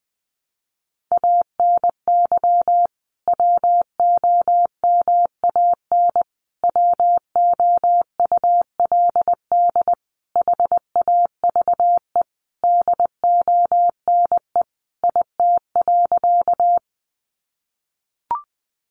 Morse code, heard as a keyed tone, about 20 wpm, 700 Hz. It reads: ANY WOMAN WOULD HAVE DONE IT.
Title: Sentences from Top 1100 Words 20wpm